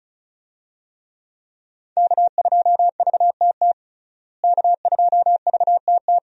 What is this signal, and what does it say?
Morse code, heard as a keyed tone, about 34 wpm, 700 Hz. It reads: K2VTT K2VTT
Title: USA Call Signs - 5 Characters - Repeated 2x - 35wpm